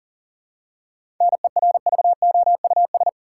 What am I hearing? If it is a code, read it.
Morse code, heard as a keyed tone, about 40 wpm, 700 Hz. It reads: NERVOUS
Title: Seven Letter Words - 40wpm